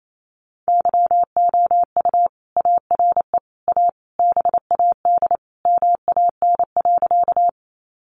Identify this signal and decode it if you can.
Morse code, heard as a keyed tone, about 28 wpm, 700 Hz. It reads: YOU ARE A BAD MAN.
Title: Sentences from Top 700 Words 28wpm